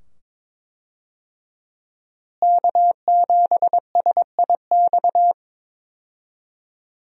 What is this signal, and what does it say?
Morse code, heard as a keyed tone, about 21 wpm, 700 Hz. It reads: K7SIX
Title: USA Call Signs - 5 Characters - Repeated 2x - 22wpm